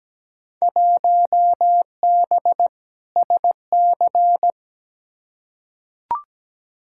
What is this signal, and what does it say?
Morse code, heard as a keyed tone, about 17 wpm, 700 Hz. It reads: 1B SC
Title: Field Day Exchanges - 17wpm